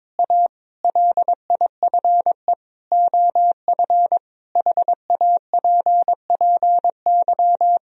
Morse code, heard as a keyed tone, about 22 wpm, 700 Hz. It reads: A LIFE OF HAPPY
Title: Sentences from Top 1000 Words - Review Part 4 - 22wpm